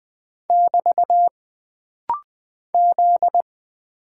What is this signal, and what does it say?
Morse code, heard as a keyed tone, about 20 wpm, 700 Hz. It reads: <BT> Z
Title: Single Character (Letter:Number:Punctuation:Prosign) - Rapid-Fire - 20wpm